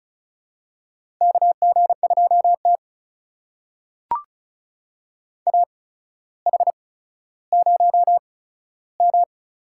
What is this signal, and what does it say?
Morse code, heard as a keyed tone, about 36 wpm, spaced as for 15 wpm, 700 Hz. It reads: KG2T AH0M
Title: USA Call Signs - 4 Characters - 8x Character Spacing - 35wpm